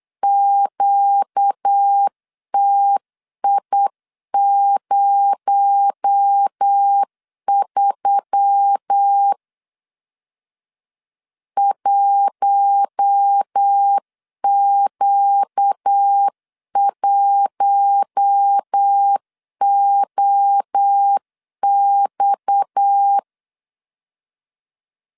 Morse code, heard as a keyed tone, about 9 wpm, 800 Hz. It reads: QTI03 1Q1OX
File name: CW / DET / 1 6 1500C S C 8wpm 5219